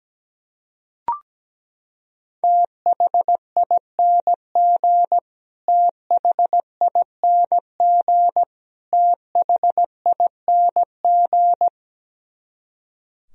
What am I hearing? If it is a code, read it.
Morse code, heard as a keyed tone, about 17 wpm, 700 Hz. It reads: THING THING THING
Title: Top 100 Words Repeated 17wpm